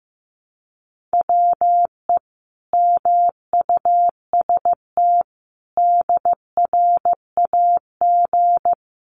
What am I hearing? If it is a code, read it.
Morse code, heard as a keyed tone, about 14 wpm, 700 Hz. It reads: WE MUST DRAG
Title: Sentences from Top 1500 Words 15wpm